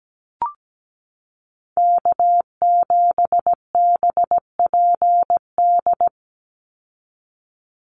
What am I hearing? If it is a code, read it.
Morse code, heard as a keyed tone, about 17 wpm, 700 Hz. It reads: K7BPD
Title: USA Call Signs 17wpm